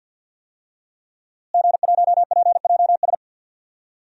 Morse code, heard as a keyed tone, about 53 wpm, 700 Hz. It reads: G1PJS
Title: DXCC-100 Call Signs - 5 Characters - Repeated 3x - 50wpm